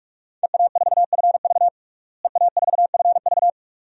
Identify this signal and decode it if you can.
Morse code, heard as a keyed tone, about 47 wpm, 700 Hz. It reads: EA4FV EA4FV
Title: DXCC-100 Call Signs - 5 Characters - Repeated 2x - 45wpm